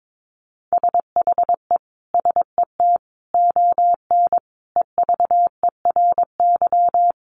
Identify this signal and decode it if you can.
Morse code, heard as a keyed tone, about 22 wpm, 700 Hz. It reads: SHE SET ON EVERY
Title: Sentences from Top 1300 Words 22wpm